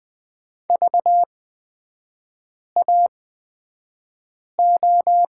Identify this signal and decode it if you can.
Morse code, heard as a keyed tone, about 20 wpm, spaced as for 5 wpm, 700 Hz. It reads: VAO
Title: 236 Lesson 38e - Letter q - Three Characters 20wpm